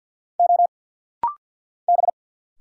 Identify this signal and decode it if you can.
Morse code, heard as a keyed tone, about 50 wpm, 700 Hz. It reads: O B
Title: Single Letter-Number - Rapid-Fire 50wpm